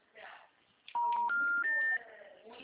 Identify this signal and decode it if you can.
Special information tone, with speech.